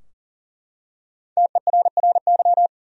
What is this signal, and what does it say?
Morse code, heard as a keyed tone, about 40 wpm, 700 Hz. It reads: TERRY
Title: USA Common Names - Max 5 Letters - 40wpm